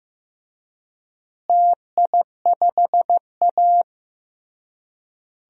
Morse code, heard as a keyed tone, about 15 wpm, 700 Hz. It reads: TI5A
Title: DXCC-100 Call Signs - 4 Characters - Repeated 3x - 15wpm